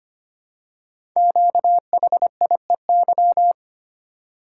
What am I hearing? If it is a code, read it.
Morse code, heard as a keyed tone, about 25 wpm, 690 Hz. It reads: QHIEY
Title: Code Groups - 5 Letters - 25wpm